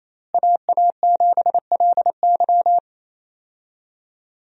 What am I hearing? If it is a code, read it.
Morse code, heard as a keyed tone, about 28 wpm, 700 Hz. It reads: AA7LY
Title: USA Call Signs - 5 Characters - Repeated 3x - 28wpm